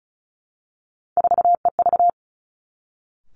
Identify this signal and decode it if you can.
Morse code, heard as a keyed tone, about 36 wpm, 700 Hz. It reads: 4EV
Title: Three Challenging Letter-Number Combinations 35wpm